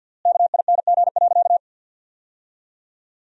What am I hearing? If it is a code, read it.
Morse code, heard as a keyed tone, about 50 wpm, 700 Hz. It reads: KING.
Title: 3x Word-Spacing - Sentences from Top 400 Words - 50wpm